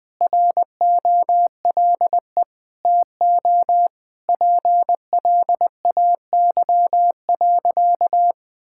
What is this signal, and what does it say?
Morse code, heard as a keyed tone, about 20 wpm, 700 Hz. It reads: ROLE TO PLAY.
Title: Sentences from Top 1000 Words - Review Part 6 - 20wpm